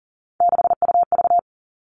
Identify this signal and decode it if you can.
Morse code, heard as a keyed tone, about 40 wpm, 700 Hz. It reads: 6UV
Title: All Cut Numbers - 3-Digit 40wpm